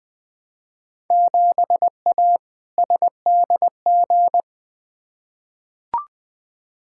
Morse code, heard as a keyed tone, about 20 wpm, 700 Hz. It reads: 7A SDG
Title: Field Day Exchanges Encore - 20wpm